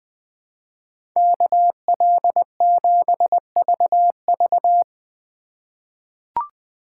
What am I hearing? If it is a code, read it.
Morse code, heard as a keyed tone, about 20 wpm, 700 Hz. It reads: KL7VV
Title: USA Call Signs - 5 Characters - 2x Character Spacing - 20wpm